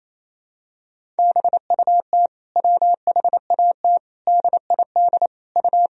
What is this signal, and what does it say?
Morse code, heard as a keyed tone, about 27 wpm, 700 Hz. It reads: BUT WHAT DID U
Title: Sentences from Top 200 Words and Abbreviations 28wpm